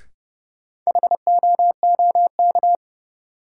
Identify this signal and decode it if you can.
Morse code, heard as a keyed tone, about 32 wpm, 700 Hz. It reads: HOOK
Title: Sentences from Top 1600 Words - New Word Drill - 30wpm